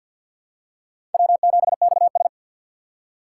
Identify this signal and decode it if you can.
Morse code, heard as a keyed tone, about 49 wpm, 700 Hz. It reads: W7XS
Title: USA Call Signs - 4x Character Spacing - 50wpm